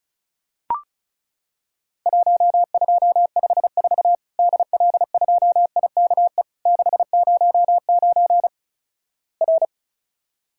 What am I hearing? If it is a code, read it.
Morse code, heard as a keyed tone, about 36 wpm, 700 Hz. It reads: 1254 DL2IKE 609 R
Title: Worked All Europe DX Contest - QTC Exchange - 35wpm